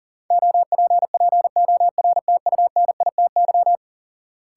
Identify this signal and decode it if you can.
Morse code, heard as a keyed tone, about 40 wpm, 700 Hz. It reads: OPPORTUNITY
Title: Sentences from Top 1000 Words - New Word Drill - 40wpm